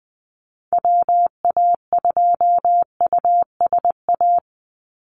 Morse code, heard as a keyed tone, about 20 wpm, 700 Hz. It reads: WA2USA
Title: NA SOTA Chaser Callsigns 20wpm